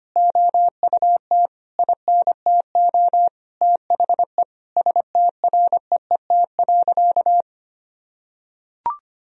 Morse code, heard as a keyed tone, about 25 wpm, 700 Hz. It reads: OUT INTO THE STREET.
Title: Sentences from Top 1000 Words - Review Part 3 - 25wpm